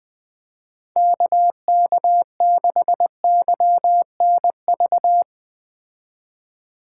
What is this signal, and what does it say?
Morse code, heard as a keyed tone, about 19 wpm, 700 Hz. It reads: KK6YNV